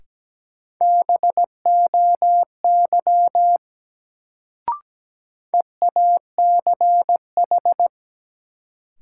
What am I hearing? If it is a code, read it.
Morse code, heard as a keyed tone, about 17 wpm, 700 Hz. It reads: BOY EACH
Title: Sentences from Top 200 Words - New Word Drill - 17wpm